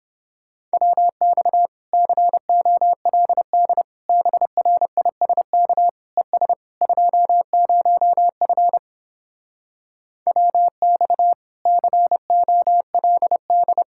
Morse code, heard as a keyed tone, about 31 wpm, 700 Hz. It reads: WX COLD BRISK ES 20F WX COLD